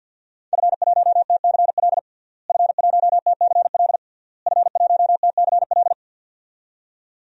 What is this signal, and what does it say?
Morse code, heard as a keyed tone, about 52 wpm, 700 Hz. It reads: F1TCL F1TCL F1TCL
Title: DXCC-100 Call Signs - 5 Characters - Repeated 3x - 50wpm